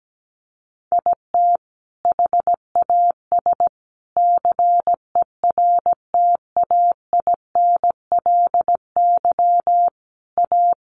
Morse code, heard as a keyed tone, about 17 wpm, 700 Hz. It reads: IT HAS CERTAINLY A
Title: Sentences from Top 1200 Words 17wpm